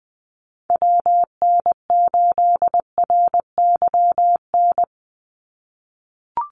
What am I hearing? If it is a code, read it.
Morse code, heard as a keyed tone, about 20 wpm, 700 Hz. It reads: WN8RYN
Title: USA Call Signs 20wpm